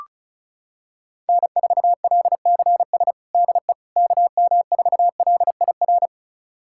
Morse code, heard as a keyed tone, about 34 wpm, 700 Hz. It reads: N4LCS DE KM4LIR
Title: USA Callsign DE Callsign 35wpm